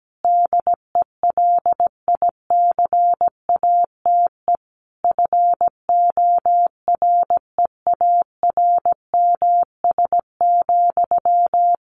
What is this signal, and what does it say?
Morse code, heard as a keyed tone, about 17 wpm, 700 Hz. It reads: DELICATE FOREARMS,
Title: A Sound Of Thunder 17wpm